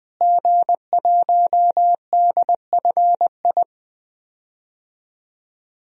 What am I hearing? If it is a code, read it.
Morse code, heard as a keyed tone, about 20 wpm, 700 Hz. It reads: G1DFI
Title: DXCC-100 Call Signs - 5 Characters - Repeated 3x - 20wpm